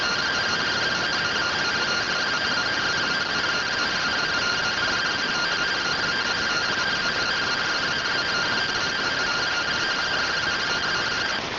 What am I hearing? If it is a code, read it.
Two-tone teleprinter signal, about 170 Hz shift.
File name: findings / CT AXIAL -111.gen-21 rtty rx